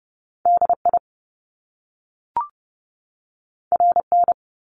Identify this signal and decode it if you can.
Morse code, heard as a keyed tone, about 30 wpm, 700 Hz. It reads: DI RN
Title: Top Two Letter Combinations in Words 30wpm